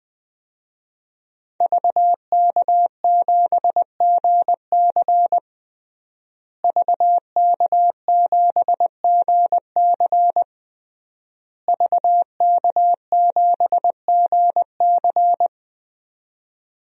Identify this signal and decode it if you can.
Morse code, heard as a keyed tone, about 20 wpm, 700 Hz. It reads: VK7GC VK7GC VK7GC
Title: DXCC-100 Call Signs - 5 Characters - Repeated 3x - 20wpm